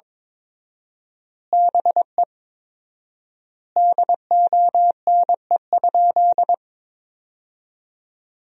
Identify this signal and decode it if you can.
Morse code, heard as a keyed tone, about 22 wpm, 700 Hz. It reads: BE DONE?
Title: 4x Word-Spacing - Sentences from Top 500 Words - 22wpm